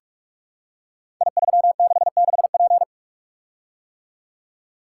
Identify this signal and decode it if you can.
Morse code, heard as a keyed tone, about 44 wpm, 700 Hz. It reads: I366P
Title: Code Groups - 5 Letter:Numbers - 45wpm